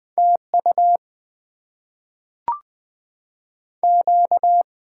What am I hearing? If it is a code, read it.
Morse code, heard as a keyed tone, about 20 wpm, 700 Hz. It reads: TU Q
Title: Top 90 QSO Elements - 6x Character Spacing - 20wpm